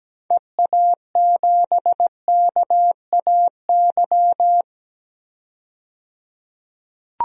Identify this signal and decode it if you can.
Morse code, heard as a keyed tone, about 17 wpm, 700 Hz. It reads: EA7KAY
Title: DXCC-100 Call Signs - 6 Characters - 17wpm